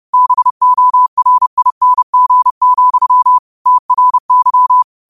Morse code, heard as a keyed tone, about 30 wpm, 1000 Hz. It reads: BORING, TRY
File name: qotdfiles / 30wpm